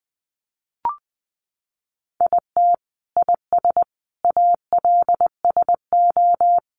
Morse code, heard as a keyed tone, about 20 wpm, 700 Hz. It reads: IT IS ALSO
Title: Sentences from Top 1400 Words 20wpm